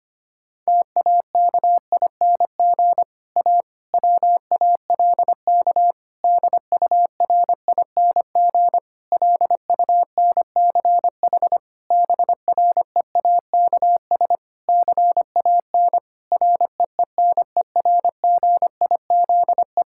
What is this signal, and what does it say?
Morse code, heard as a keyed tone, about 25 wpm, 700 Hz. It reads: TAKING A WALK DURING LUNCH BREAKS CAN REENERGIZE
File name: practice / 25wpm / Difficult Sentences with Hints - 25wpm